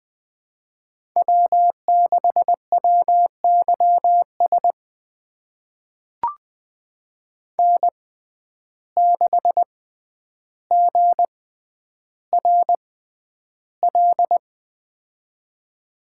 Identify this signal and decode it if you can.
Morse code, heard as a keyed tone, about 20 wpm, spaced as for 10 wpm, 700 Hz. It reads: W6WYS N6GRL
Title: USA Call Signs - 5 Characters - 6x Character Spacing - 20wpm